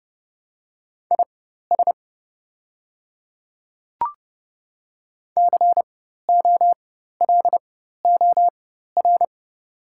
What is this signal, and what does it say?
Morse code, heard as a keyed tone, about 30 wpm, spaced as for 14 wpm, 690 Hz. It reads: IS COLOR
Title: 4x Character Spacing - Top 300 Words - 30wpm